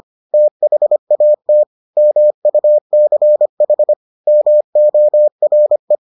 Morse code, heard as a keyed tone, about 25 wpm, 600 Hz. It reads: THAT MUCH MORE